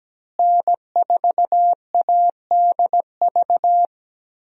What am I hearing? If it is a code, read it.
Morse code, heard as a keyed tone, about 17 wpm, 700 Hz. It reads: N4ADV
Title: USA Call Signs - 5 Characters - Repeated 3x - 17wpm